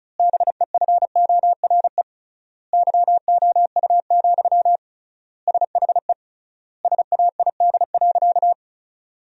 Morse code, heard as a keyed tone, about 36 wpm, 700 Hz. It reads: BEFORE YOU, SHE SAID.
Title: 3x Word-Spacing - Sentences from Top 400 Words - 35wpm